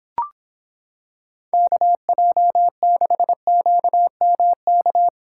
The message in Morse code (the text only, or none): KJ6QMK